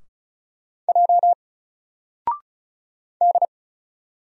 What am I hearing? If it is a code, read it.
Morse code, heard as a keyed tone, about 36 wpm, 700 Hz. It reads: J D
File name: Single-Letter-Number-35wpm